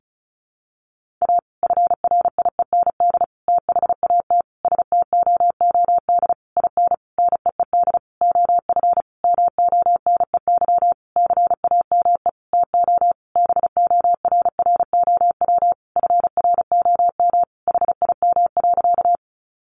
Morse code, heard as a keyed tone, about 34 wpm, 700 Hz. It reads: A FRIEND THAT STOOD IN NEED OF MONEY CAME TO BORROW FROM HIM.
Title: Sentences from Top 1500 Words 35wpm